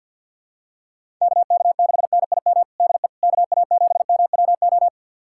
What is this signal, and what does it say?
Morse code, heard as a keyed tone, about 52 wpm, 700 Hz. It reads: KK6NIM DE KA7MWO
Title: USA Callsign DE Callsign - 6 Characters - 50wpm